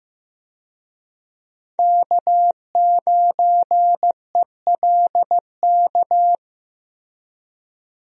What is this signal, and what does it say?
Morse code, heard as a keyed tone, about 15 wpm, 700 Hz. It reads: K9ELK
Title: USA Call Signs - 5 Characters - Repeated 2x - 15wpm